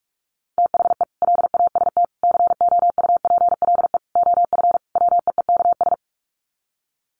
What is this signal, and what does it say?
Morse code, heard as a keyed tone, about 44 wpm, 700 Hz. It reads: THE LAST COUPLE OF WEEKS
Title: Sets of 5 Words Encore2 45wpm